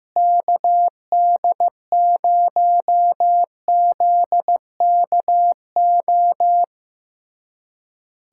Morse code, heard as a keyed tone, about 15 wpm, 700 Hz. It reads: KD0ZKO
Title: USA Callsign DE Callsign - 6 Characters - 15wpm